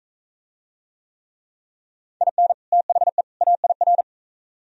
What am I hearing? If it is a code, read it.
Morse code, heard as a keyed tone, about 41 wpm, 700 Hz. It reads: IN THE AIR